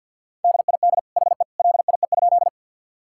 Morse code, heard as a keyed tone, about 50 wpm, 700 Hz. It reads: DID HE LIE?
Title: Sentences from Top 1600 Words 50wpm